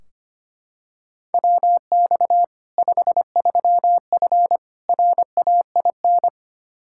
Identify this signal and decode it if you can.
Morse code, heard as a keyed tone, about 25 wpm, 700 Hz. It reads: WX 53F RAIN
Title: Practice QSO - Exchange 2 - Info - 25wpm